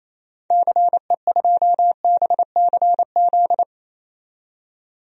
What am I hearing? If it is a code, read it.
Morse code, heard as a keyed tone, about 27 wpm, 700 Hz. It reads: CE2BCZ
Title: DXCC-100 Call Signs - 6 Characters - 28wpm